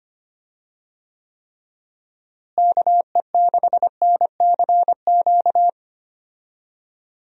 Morse code, heard as a keyed tone, about 25 wpm, 700 Hz. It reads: KE6NCQ
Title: USA Call Signs - 6 Characters - Repeated 2x - 25wpm